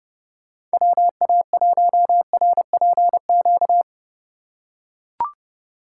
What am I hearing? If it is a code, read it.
Morse code, heard as a keyed tone, about 30 wpm, 700 Hz. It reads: WA1RPQ